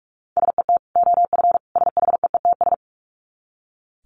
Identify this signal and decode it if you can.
Morse code, heard as a keyed tone, about 44 wpm, 700 Hz. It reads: SET OF SHEETS
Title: Dit Challenge 45wpm